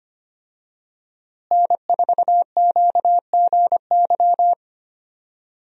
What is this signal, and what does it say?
Morse code, heard as a keyed tone, about 25 wpm, 700 Hz. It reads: N4QGY
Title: USA Call Signs - 5 Characters - Repeated 2x - 25wpm